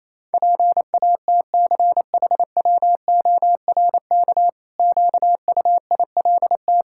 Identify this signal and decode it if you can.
Morse code, heard as a keyed tone, about 27 wpm, 700 Hz. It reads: PATCHWORK QUILT
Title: Pangram Challenge v2 - 28wpm